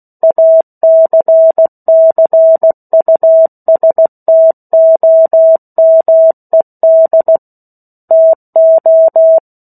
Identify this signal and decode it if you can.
Morse code, heard as a keyed tone, about 16 wpm, 650 Hz. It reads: ACCUSTOMED TO
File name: 03-Chapter-03 16 WPM